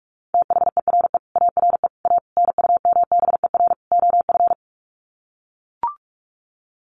Morse code, heard as a keyed tone, about 44 wpm, 700 Hz. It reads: THERE ARE A NUMBER OF
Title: Sets of 5 Words 45wpm